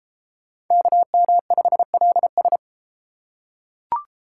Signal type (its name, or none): Morse code